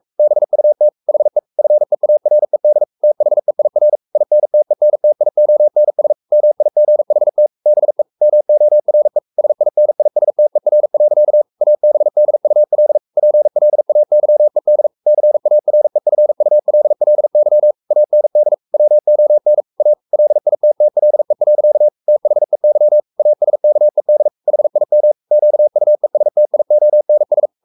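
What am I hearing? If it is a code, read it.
Morse code, heard as a keyed tone, about 44 wpm, 600 Hz. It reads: BUT HE FEARED THEIR INTENTIONS MIGHT BE MORE SINISTER. ABDUL PLAYED CAREFULLY AND WON A LITTLE. THEY ASKED HIM QUESTIONS
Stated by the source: Morse code at 43 wpm sending Ken Follett's novel Never in English, from CW med Gnister.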